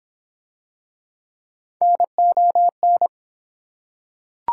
Morse code, heard as a keyed tone, about 25 wpm, 700 Hz. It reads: NON